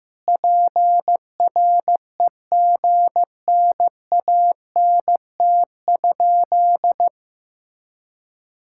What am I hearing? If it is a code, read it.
Morse code, heard as a keyed tone, about 16 wpm, 700 Hz. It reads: PREGNANT?
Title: Sentences from Top 1000 Words - Review Part 6 - 15wpm